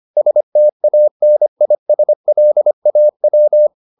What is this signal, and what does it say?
Morse code, heard as a keyed tone, about 25 wpm, 600 Hz. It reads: STANISLAW